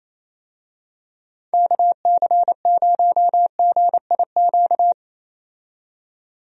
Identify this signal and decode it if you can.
Morse code, heard as a keyed tone, about 28 wpm, 700 Hz. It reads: KC0GIQ
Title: USA Call Signs - 6 Characters - 28wpm